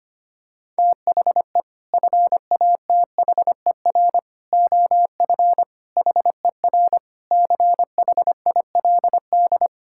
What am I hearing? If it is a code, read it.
Morse code, heard as a keyed tone, about 25 wpm, 700 Hz. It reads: THE FATHER OF HER CHILD
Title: Sentences from Top 1000 Words - Review Part 5 - 25wpm